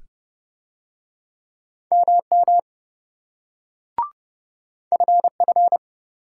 Morse code, heard as a keyed tone, about 27 wpm, 700 Hz. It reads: MM FF